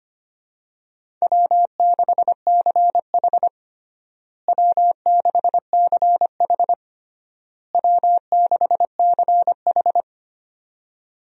Morse code, heard as a keyed tone, about 25 wpm, 700 Hz. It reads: W6CH W6CH W6CH
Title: USA Call Signs - 4 Characters - Repeated 3x - 25wpm